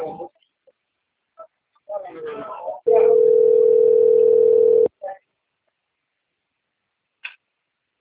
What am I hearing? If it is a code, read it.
Ringback tone (North America), 440 + 480 Hz, with speech.
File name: rg-1600-unknown- - - .22339